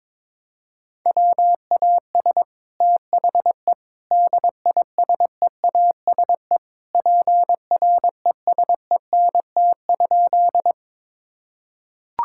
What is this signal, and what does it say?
Morse code, heard as a keyed tone, about 21 wpm, 700 Hz. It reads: WAS THE DISEASE PRESENT?
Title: Sentences from Top 1000 Words - Review Part 6 - 22wpm